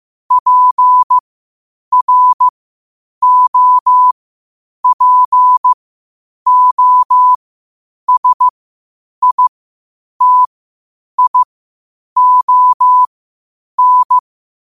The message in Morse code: PROPOSITION

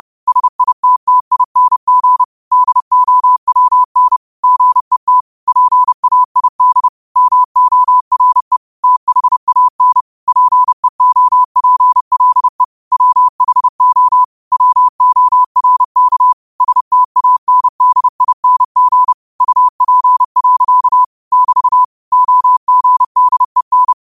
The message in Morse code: SITTING DOWN GET PAID MORE THAN PEOPLE WHO WORK STANDING UP. <BT> OGDEN

